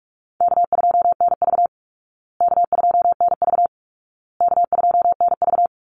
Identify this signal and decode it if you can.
Morse code, heard as a keyed tone, about 44 wpm, 700 Hz. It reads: K2NV K2NV K2NV
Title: Call Signs Repeated 2x - 45wpm